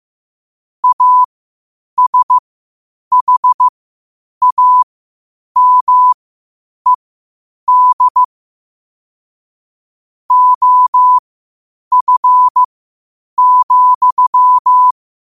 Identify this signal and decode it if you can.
Morse code, heard as a keyed tone, about 15 wpm, spaced as for 8 wpm, 1000 Hz. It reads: ASHAMED OF,